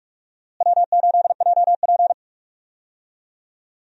Morse code, heard as a keyed tone, about 46 wpm, 700 Hz. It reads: W8JP
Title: USA Call Signs - 4 Characters - Repeated 3x - 45wpm